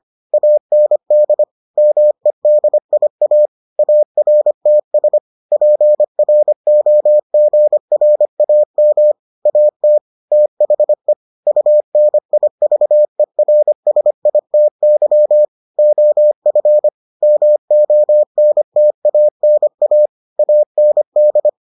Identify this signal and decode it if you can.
Morse code, heard as a keyed tone, about 25 wpm, 600 Hz. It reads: AND MEDIA ARTS PROGRAM AT THE UNIVERSITY OF MONTANA AND